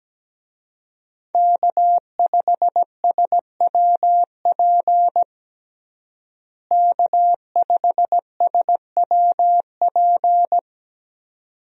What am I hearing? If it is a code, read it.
Morse code, heard as a keyed tone, about 17 wpm, 700 Hz. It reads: K5SWP K5SWP
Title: USA Call Signs - 5 Characters - Repeated 2x - 17wpm